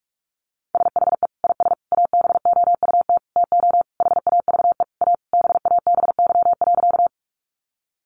Morse code, heard as a keyed tone, about 44 wpm, 700 Hz. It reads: SHE IS ABOUT TO HAVE A BABY.
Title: Sentences from Top 1300 Words 45wpm